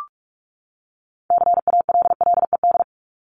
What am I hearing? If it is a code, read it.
Morse code, heard as a keyed tone, about 44 wpm, 700 Hz. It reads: CALLED
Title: ED Suffix - Top 100 Words with Base Words 45wpm